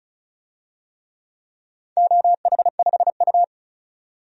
Morse code, heard as a keyed tone, about 36 wpm, 700 Hz. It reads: OH5U
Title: DXCC-100 Call Signs - 4 Characters - 35wpm